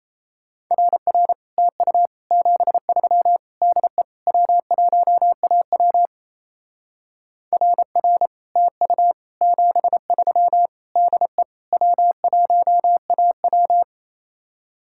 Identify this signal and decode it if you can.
Morse code, heard as a keyed tone, about 34 wpm, 700 Hz. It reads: RR TU 73 DE W1AW RR TU 73 DE W1AW
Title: Speed-Racing Common QSO Phrases Encore - 28wpm